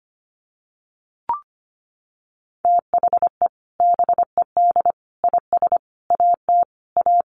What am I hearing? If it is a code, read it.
Morse code, heard as a keyed tone, about 25 wpm, 700 Hz. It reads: THE BED IS AT A
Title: Sentences from Top 1500 Words 25wpm